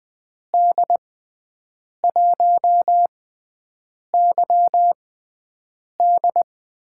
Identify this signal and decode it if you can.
Morse code, heard as a keyed tone, about 20 wpm, spaced as for 7 wpm, 700 Hz. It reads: D1YD
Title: USA Call Signs - 5 Characters - 6x Character Spacing - 20wpm